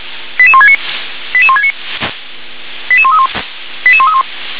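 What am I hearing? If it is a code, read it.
Five-tone selective call (ZVEI), heard as quick runs of five tones, tones of about 80 ms. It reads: 88169 88169 88131 88131
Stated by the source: Received on SDR, ZVEI five-tone selective call 88277.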